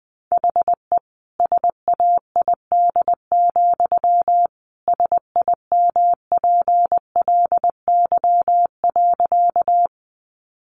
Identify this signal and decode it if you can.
Morse code, heard as a keyed tone, about 20 wpm, 700 Hz. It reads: HE SAID, SIMPLY.
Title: Sentences from Top 600 Words 20wpm